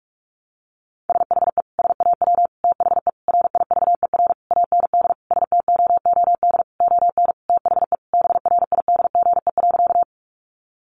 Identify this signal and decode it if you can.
Morse code, heard as a keyed tone, about 44 wpm, 700 Hz. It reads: SHE SAW THE RIVER AND STOOD ON THE BRIDGE.
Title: Sentences from Top 1200 Words 45wpm